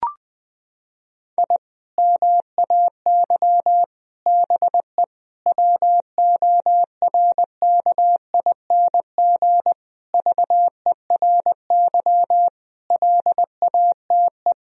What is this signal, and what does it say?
Morse code, heard as a keyed tone, about 20 wpm, 700 Hz. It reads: I MAY BE WORKING VERY LATE
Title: Sentences from Top 1000 Words - Review Part 4 - 20wpm